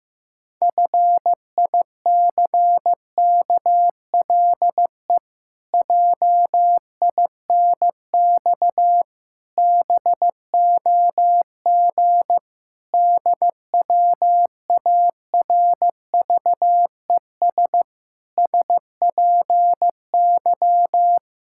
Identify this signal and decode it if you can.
Morse code, heard as a keyed tone, about 14 wpm, 700 Hz. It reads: FICKLE JINX BOG DWARVES SPY
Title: Pangram Challenge v2 - 15wpm